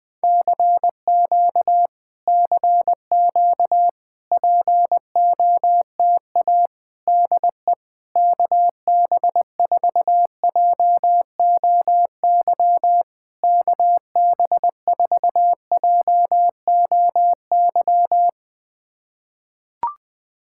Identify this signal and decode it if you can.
Morse code, heard as a keyed tone, about 20 wpm, 690 Hz. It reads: CQ CQ POTA DE KB4JOY KB4JOY
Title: POTA QSO Practice - 20wpm